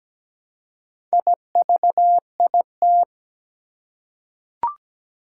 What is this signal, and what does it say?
Morse code, heard as a keyed tone, about 17 wpm, 700 Hz. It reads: IVIT
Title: Four Challenging Letter-Number Combinations 17wpm